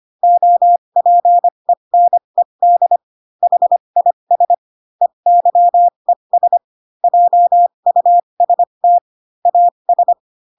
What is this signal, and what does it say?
Morse code, heard as a keyed tone, about 25 wpm, spaced as for 22 wpm, 700 Hz. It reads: OPENED HIS EYES JUST AS
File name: Pinocchio / Pinocchio ch6 18 25 ik1hge